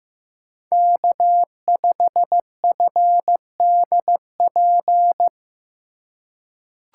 Morse code, heard as a keyed tone, about 15 wpm, 700 Hz. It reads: K5FDP
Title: USA Call Signs - 5 Characters - 2x Character Spacing - 15wpm